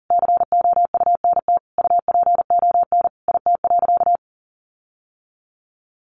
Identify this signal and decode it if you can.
Morse code, heard as a keyed tone, about 40 wpm, 700 Hz. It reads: COUNT UPON IT.
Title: Sentences from Top 1500 Words 40wpm